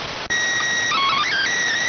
SSTV (slow-scan television), PD 290.